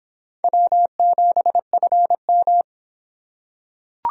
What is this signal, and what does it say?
Morse code, heard as a keyed tone, about 25 wpm, 700 Hz. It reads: W7FM